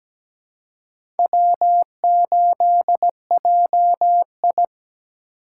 Morse code, heard as a keyed tone, about 17 wpm, 700 Hz. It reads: W8JI
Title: USA Call Signs - 4 Characters - 2x Character Spacing - 17wpm